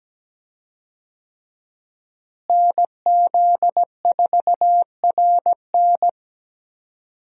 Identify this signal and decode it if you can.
Morse code, heard as a keyed tone, about 17 wpm, 700 Hz. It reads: NZ4RN
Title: USA Call Signs - 5 Characters - Repeated 2x - 17wpm